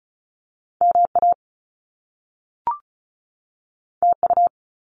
Morse code, heard as a keyed tone, about 36 wpm, 700 Hz. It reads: MA TU
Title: Top Two Letter Combinations in Words 35wpm